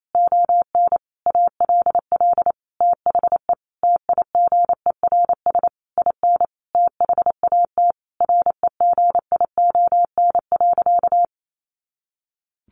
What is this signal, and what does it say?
Morse code, heard as a keyed tone, about 28 wpm, 700 Hz. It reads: ON ALL THE TIGERS IN THAT REGION.
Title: A Sound Of Thunder 28wpm